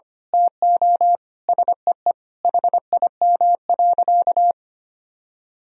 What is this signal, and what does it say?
Morse code, heard as a keyed tone, about 25 wpm, 700 Hz. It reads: TO SEE HIM.
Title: Sentences from Top 1000 Words - Review Part 1 - 25wpm